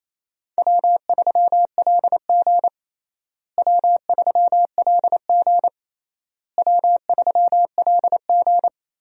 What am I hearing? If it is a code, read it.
Morse code, heard as a keyed tone, about 28 wpm, 700 Hz. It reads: W3LG W3LG W3LG
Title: USA Call Signs - 4 Characters - Repeated 3x - 28wpm